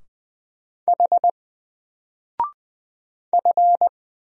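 Morse code, heard as a keyed tone, about 20 wpm, 700 Hz. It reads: H F
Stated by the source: Single letter and number Morse drill at 20 wpm, from N4LCD.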